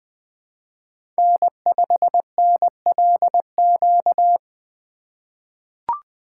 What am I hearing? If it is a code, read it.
Morse code, heard as a keyed tone, about 20 wpm, 700 Hz. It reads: N5NLQ
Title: USA Call Signs - 8x Character Spacing - 20wpm